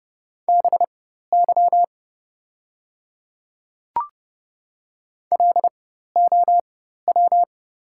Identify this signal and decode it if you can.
Morse code, heard as a keyed tone, about 30 wpm, spaced as for 14 wpm, 700 Hz. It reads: BY LOW
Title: 4x Character Spacing - Top 200 Words - 30wpm